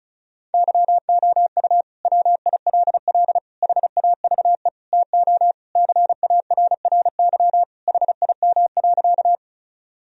Morse code, heard as a keyed tone, about 36 wpm, 700 Hz. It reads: YOU WILL HAVE TO CARRY HIM.
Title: Sentences from Top 1000 Words - Review Part 1 - 35wpm